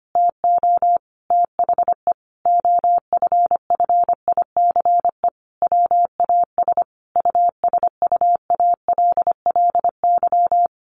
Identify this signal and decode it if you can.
Morse code, heard as a keyed tone, about 25 wpm, 700 Hz. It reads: TO THE OFFICE WAS USUALLY
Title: Sentences from Top 1500 Words 25wpm